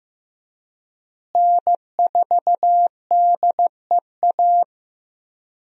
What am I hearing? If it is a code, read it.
Morse code, heard as a keyed tone, about 15 wpm, 700 Hz. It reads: N4DEA
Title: USA Call Signs - 5 Characters - Repeated 3x - 15wpm